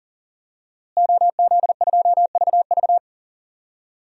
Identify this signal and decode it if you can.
Morse code, heard as a keyed tone, about 40 wpm, 700 Hz. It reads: OZ2VV
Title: DXCC-100 Call Signs - 5 Characters - Repeated 2x - 40wpm